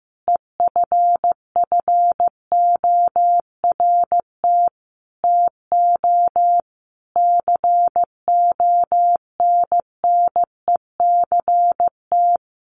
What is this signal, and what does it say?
Morse code, heard as a keyed tone, about 15 wpm, 700 Hz. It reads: EFFORT TO CONNECT
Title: Sentences from Top 1500 Words 15wpm